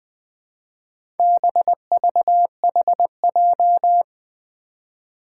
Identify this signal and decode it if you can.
Morse code, heard as a keyed tone, about 20 wpm, 700 Hz. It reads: BVHJ
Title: Four Challenging Letter-Number Combinations 20wpm